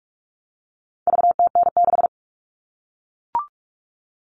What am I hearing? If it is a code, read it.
Morse code, heard as a keyed tone, about 44 wpm, 700 Hz. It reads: VTN6
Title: All Cut Numbers - 4-Digit 45wpm